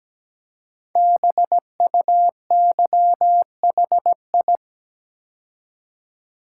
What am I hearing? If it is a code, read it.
Morse code, heard as a keyed tone, about 17 wpm, 700 Hz. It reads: BUYHI
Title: Code Groups - 5 Letters - 17wpm